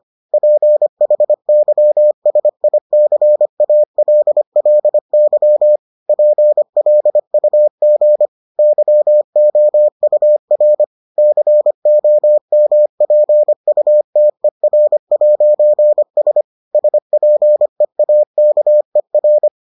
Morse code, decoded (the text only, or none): PHYSICALLY PLUG YOUR COMPUTER'S SPEAKER